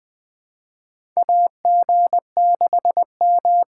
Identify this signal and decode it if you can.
Morse code, heard as a keyed tone, about 21 wpm, 700 Hz. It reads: AG6M
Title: USA Call Signs - 6x Character Spacing - 20wpm